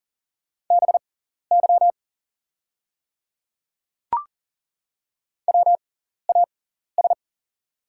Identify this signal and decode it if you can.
Morse code, heard as a keyed tone, about 38 wpm, spaced as for 14 wpm, 690 Hz. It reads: BY WAS
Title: 6x Character Spacing - Top 100 Words - 40wpm